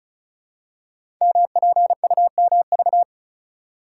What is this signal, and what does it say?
Morse code, heard as a keyed tone, about 36 wpm, 700 Hz. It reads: MPUMV
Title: Code Groups - 5 Letters - 35wpm